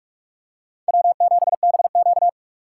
Morse code, heard as a keyed tone, about 44 wpm, 700 Hz. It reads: W7BQ